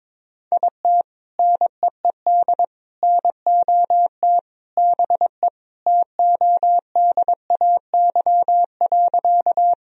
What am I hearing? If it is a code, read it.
Morse code, heard as a keyed tone, about 22 wpm, 700 Hz. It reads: IT NEED NOT BE TODAY.
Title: Sentences from Top 1000 Words - Review Part 4 - 22wpm